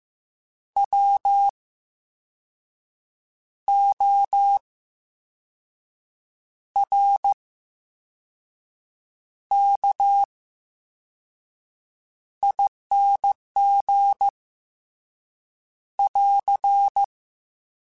Morse code, heard as a keyed tone, about 15 wpm, spaced as for 5 wpm, 800 Hz. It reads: WORKING <AR>
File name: Special INGs-mp3s / Special INGs-mp3s / VerySlowToFaster Ing @13